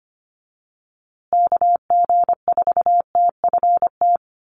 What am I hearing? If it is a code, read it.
Morse code, heard as a keyed tone, about 25 wpm, 700 Hz. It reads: KG4TFT
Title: USA Call Signs Repeated 2x 25wpm